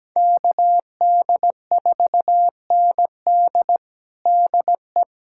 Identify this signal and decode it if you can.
Morse code, heard as a keyed tone, about 17 wpm, 700 Hz. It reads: KD4ND DE
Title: USA Callsign DE Callsign - 5 Characters - 17wpm